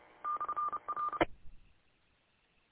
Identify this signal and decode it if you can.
Morse code, heard as a keyed tone, about 30 wpm, 1200 Hz. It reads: /R